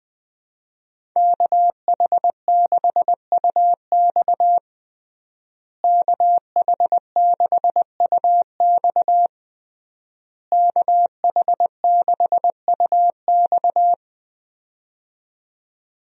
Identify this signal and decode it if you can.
Morse code, heard as a keyed tone, about 20 wpm, 700 Hz. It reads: KH6UX KH6UX KH6UX
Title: USA Call Signs - 5 Characters - Repeated 3x - 20wpm